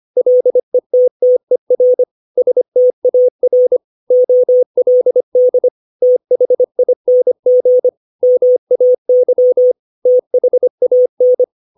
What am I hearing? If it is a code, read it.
Morse code, heard as a keyed tone, about 25 wpm, 500 Hz. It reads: LETTER STAR OLD THING MAY THAN
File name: 632 words / 632 Words 25WPM